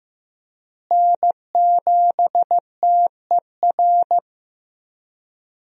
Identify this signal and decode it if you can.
Morse code, heard as a keyed tone, about 15 wpm, 700 Hz. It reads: N7TER
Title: USA Call Signs - 5 Characters - Repeated 3x - 15wpm